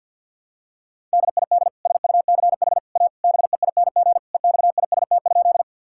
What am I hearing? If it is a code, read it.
Morse code, heard as a keyed tone, about 52 wpm, 700 Hz. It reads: DID SUCH A BEING EXIST?
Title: Sentences from Top 1000 Words - Review Part 5 - 50wpm